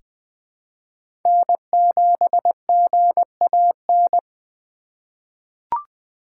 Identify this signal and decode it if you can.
Morse code, heard as a keyed tone, about 20 wpm, 700 Hz. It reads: N7GAN
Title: USA Call Signs - 4x Character Spacing - 20wpm